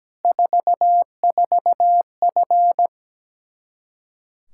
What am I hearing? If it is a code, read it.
Morse code, heard as a keyed tone, about 17 wpm, 700 Hz. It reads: 44F
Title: Practice QSO - Exchange 2 - Info - 17wpm